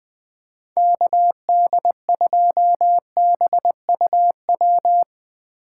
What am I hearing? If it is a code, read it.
Morse code, heard as a keyed tone, about 20 wpm, 700 Hz. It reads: KD2BUW
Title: USA Call Signs - 6 Characters - 3x Character Spacing - 20wpm